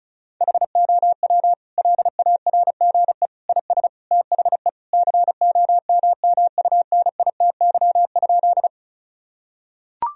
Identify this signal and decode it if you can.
Morse code, heard as a keyed tone, about 36 wpm, 700 Hz. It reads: HOW LARGE IS THE COMMUNITY?
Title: Sentences from Top 1000 Words - Review Part 4 - 35wpm